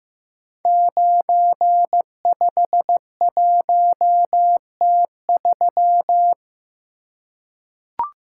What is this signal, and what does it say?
Morse code, heard as a keyed tone, about 15 wpm, 700 Hz. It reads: 951T3
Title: Code Groups - 5 Letter:Numbers - 15wpm